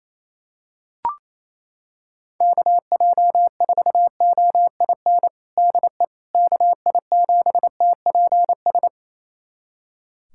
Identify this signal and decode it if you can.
Morse code, heard as a keyed tone, about 28 wpm, 700 Hz. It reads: KJ4OIN DE KI7TPS
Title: USA Callsign DE Callsign 28wpm